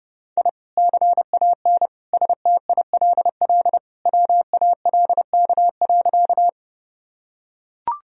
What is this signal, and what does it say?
Morse code, heard as a keyed tone, about 30 wpm, 700 Hz. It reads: I CAN STILL WALK.
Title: Sentences from Top 1000 Words - Review Part 1 - 30wpm